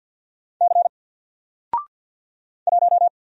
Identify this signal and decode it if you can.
Morse code, heard as a keyed tone, about 48 wpm, 700 Hz. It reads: C 1
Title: Single Letter-Number Alt 50wpm